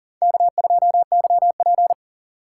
Morse code, heard as a keyed tone, about 40 wpm, 700 Hz. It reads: K2YP
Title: USA Call Signs - 4x Character Spacing - 40wpm